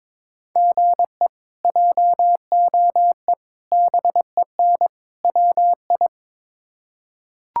Morse code, heard as a keyed tone, about 22 wpm, 700 Hz. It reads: GE JOE BEN WI
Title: K1USN SST QSO Practice - 22wpm